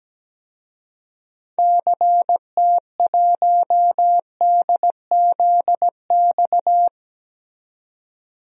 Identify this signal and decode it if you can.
Morse code, heard as a keyed tone, about 17 wpm, 700 Hz. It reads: CT1DZX
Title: DXCC-100 Call Signs - 6 Characters - 17wpm